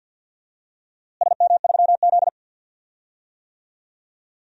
Morse code, heard as a keyed tone, about 52 wpm, 700 Hz. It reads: SM3Z
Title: DXCC-100 Call Signs - 4 Characters - Repeated 2x - 50wpm